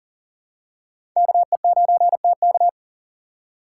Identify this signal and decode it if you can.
Morse code, heard as a keyed tone, about 38 wpm, 700 Hz. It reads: KE9TK
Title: USA Call Signs - 5 Characters - 3x Character Spacing - 40wpm